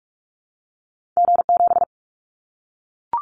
Morse code, heard as a keyed tone, about 44 wpm, 700 Hz. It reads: G7
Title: Callsign Prefixes for Top 25 DXCC - 45wpm